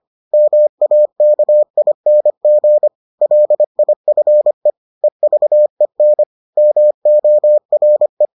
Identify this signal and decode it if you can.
Morse code, heard as a keyed tone, about 25 wpm, 600 Hz. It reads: MAKING LIFE EVEN MORE